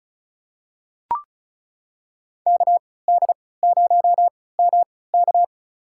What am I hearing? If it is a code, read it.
Morse code, heard as a keyed tone, about 34 wpm, spaced as for 20 wpm, 700 Hz. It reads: KD0MK
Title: USA Call Signs - 5 Characters - 3x Character Spacing - 35wpm